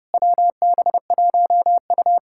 Morse code, heard as a keyed tone, about 33 wpm, 700 Hz. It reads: WB1U